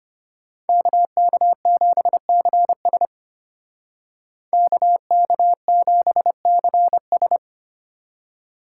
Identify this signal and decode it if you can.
Morse code, heard as a keyed tone, about 33 wpm, 700 Hz. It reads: KK7CS KK7CS